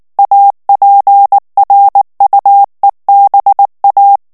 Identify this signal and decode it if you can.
Morse code, heard as a keyed tone, about 19 wpm, 800 Hz. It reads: APRUEBA
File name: ESPAGNOL TEXTE EN CLAIR 2 VITESSE 18 MOTS MINUTES